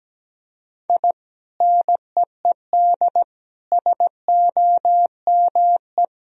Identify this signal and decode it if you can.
Morse code, heard as a keyed tone, about 17 wpm, 700 Hz. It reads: I NEED SOME
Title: Sentences from Top 1000 Words - Review Part 1 - 17wpm